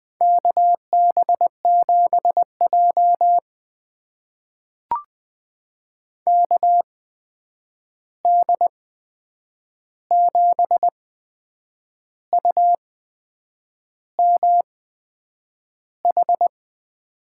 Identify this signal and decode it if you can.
Morse code, heard as a keyed tone, about 20 wpm, spaced as for 8 wpm, 700 Hz. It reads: KB7J KD7UMH
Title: USA Call Signs - 8x Character Spacing - 20wpm